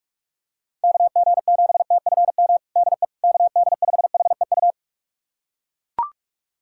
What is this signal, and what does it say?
Morse code, heard as a keyed tone, about 42 wpm, 700 Hz. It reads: KG7TFM DE KD5HEU
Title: USA Callsign DE Callsign - 6 Characters - 45wpm